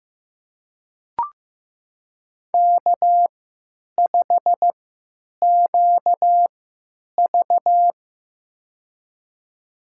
Morse code, heard as a keyed tone, about 15 wpm, spaced as for 9 wpm, 700 Hz. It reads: K5QV
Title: USA Call Signs - 4 Characters - 3x Character Spacing - 15wpm